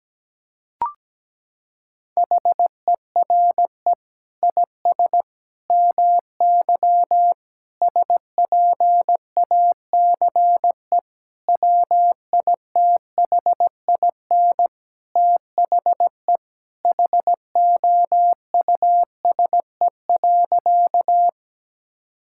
Morse code, heard as a keyed tone, about 17 wpm, 700 Hz. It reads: HERE IS MY SPACE WITHIN THE HOUSE.
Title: Sentences from Top 1000 Words - Review Part 5 - 17wpm